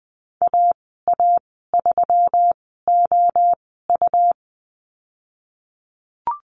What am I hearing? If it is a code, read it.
Morse code, heard as a keyed tone, about 20 wpm, spaced as for 15 wpm, 700 Hz. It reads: AA3OU
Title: 147 Lesson 24g - Letter y - Call Signs 20wpm